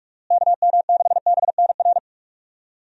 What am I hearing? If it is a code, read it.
Morse code, heard as a keyed tone, about 46 wpm, 700 Hz. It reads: KM6BNR
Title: USA Call Signs - 6 Characters - Repeated 3x - 45wpm